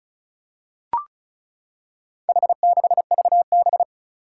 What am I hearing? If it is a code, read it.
Morse code, heard as a keyed tone, about 36 wpm, 700 Hz. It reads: H6VB
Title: Four Challenging Letter-Number Combinations 35wpm